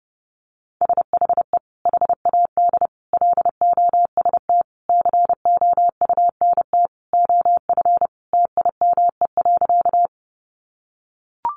Morse code, heard as a keyed tone, about 30 wpm, 700 Hz. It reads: SHE HAD LOST COUNT OF TIME.
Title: Sentences from Top 1500 Words 30wpm